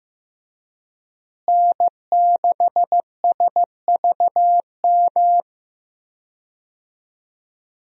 Morse code, heard as a keyed tone, about 15 wpm, 700 Hz. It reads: N6SVM
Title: USA Call Signs - 5 Characters - Repeated 3x - 15wpm